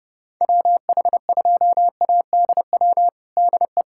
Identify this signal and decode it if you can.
Morse code, heard as a keyed tone, about 32 wpm, 700 Hz. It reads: WH2ADW DE
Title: USA Callsign DE Callsign - 6 Characters - 30wpm